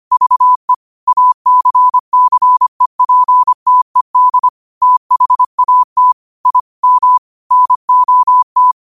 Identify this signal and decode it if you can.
Morse code, heard as a keyed tone, about 25 wpm, 1000 Hz. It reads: VE ACCEPTED THAT I M NOT
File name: 25wpm